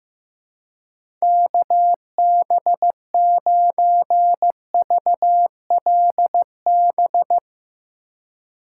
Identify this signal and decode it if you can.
Morse code, heard as a keyed tone, about 15 wpm, 700 Hz. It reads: KB9VLB
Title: USA Call Signs - 3x Character Spacing - 15wpm